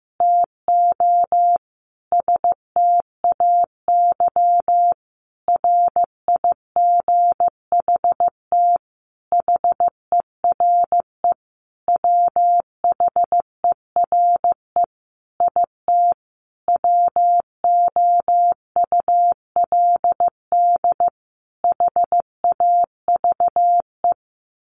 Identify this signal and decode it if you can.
Morse code, heard as a keyed tone, about 14 wpm, 700 Hz. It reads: TO STAY RIGHT HERE WHERE IT WOULD HAVE
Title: A Sound Of Thunder 15wpm